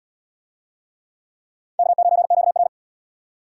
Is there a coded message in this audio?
Yes: Morse code from 1.8 to 2.7 s, heard as a keyed tone, about 72 wpm, 700 Hz. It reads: K9QR